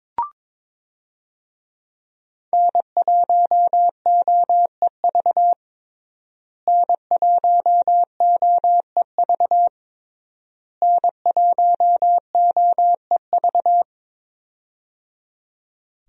Morse code, heard as a keyed tone, about 22 wpm, 700 Hz. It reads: N1OEV N1OEV N1OEV
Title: USA Call Signs - 5 Characters - Repeated 3x - 22wpm